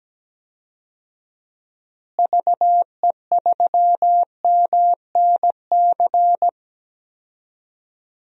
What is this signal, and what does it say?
Morse code, heard as a keyed tone, about 17 wpm, 700 Hz. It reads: VE3MNC
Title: DXCC-100 Call Signs - 6 Characters - Repeated 3x - 17wpm